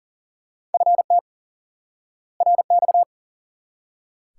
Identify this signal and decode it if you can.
Morse code, heard as a keyed tone, about 40 wpm, 700 Hz. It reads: FT RX
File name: IQE Course 3 - Lesson 89 - YAGI - Two Elements - 40wpm